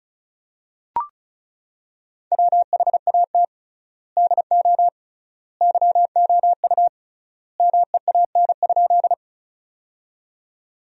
Morse code, heard as a keyed tone, about 36 wpm, 700 Hz. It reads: WHAT DO YOU MEAN?
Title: 3x Word-Spacing - Sentences from Top 200 Words - 35wpm